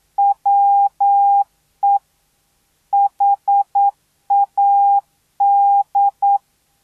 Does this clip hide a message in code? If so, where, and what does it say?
Yes: Morse code from 0.2 to 6.4 s, heard as a keyed tone, about 9 wpm, 800 Hz. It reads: WE HAD